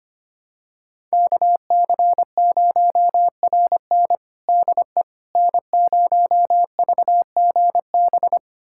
Morse code, heard as a keyed tone, about 25 wpm, 700 Hz. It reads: KC0RN DE N0VGB
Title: USA Callsign DE Callsign 25wpm